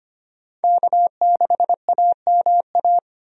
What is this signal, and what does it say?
Morse code, heard as a keyed tone, about 25 wpm, 700 Hz. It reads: K6AMA
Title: USA Call Signs - 5 Characters - Repeated 2x - 25wpm